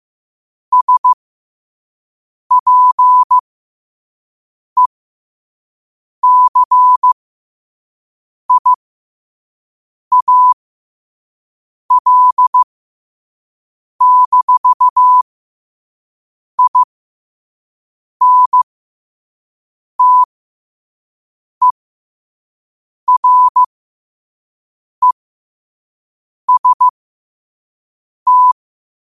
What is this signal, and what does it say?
Morse code, heard as a keyed tone, about 15 wpm, spaced as for 5 wpm, 1000 Hz. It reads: SPECIAL-INTEREST
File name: qotdfiles / 5wpm